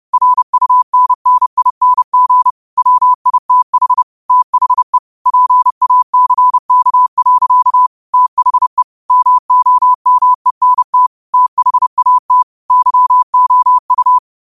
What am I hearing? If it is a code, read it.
Morse code, heard as a keyed tone, about 30 wpm, 1000 Hz. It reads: RUNNING WITH THE PACK. THE MOMENT THAT YOU
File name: qotdfiles / 30wpm